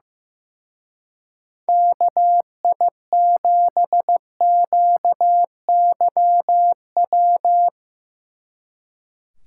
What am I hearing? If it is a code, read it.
Morse code, heard as a keyed tone, about 16 wpm, 700 Hz. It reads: KI7QYW
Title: Speed Racing - USA 2x3 Call Signs 15wpm